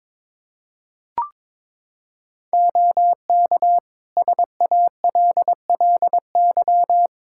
Morse code, heard as a keyed tone, about 21 wpm, 700 Hz. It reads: OK SALLY
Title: Practice QSO - Exchange 2 - Info - 22wpm